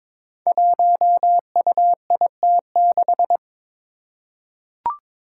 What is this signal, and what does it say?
Morse code, heard as a keyed tone, about 22 wpm, 700 Hz. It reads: 1UIT6
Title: Five Challenging Letter-Number Combinations 22wpm